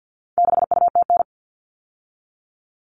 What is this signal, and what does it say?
Morse code, heard as a keyed tone, about 50 wpm, 700 Hz. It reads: 6UTN
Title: All Cut Numbers - 4-Digit 50wpm